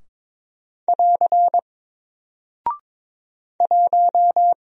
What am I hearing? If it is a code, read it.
Morse code, heard as a keyed tone, about 22 wpm, 700 Hz. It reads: <AR> 1
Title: Single Character (Letter:Number:Punctuation:Prosign) 22wpm